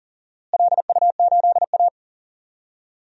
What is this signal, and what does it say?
Morse code, heard as a keyed tone, about 39 wpm, 700 Hz. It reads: LU8A
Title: DXCC-100 Call Signs - 4 Characters - Repeated 3x - 40wpm